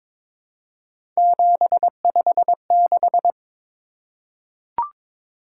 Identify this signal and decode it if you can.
Morse code, heard as a keyed tone, about 22 wpm, 700 Hz. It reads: 756
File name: practice / 22wpm / Three Letter-Number Combinations - 22wpm